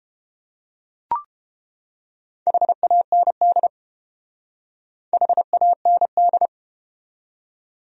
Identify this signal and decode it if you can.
Morse code, heard as a keyed tone, about 30 wpm, 700 Hz. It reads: HAND HAND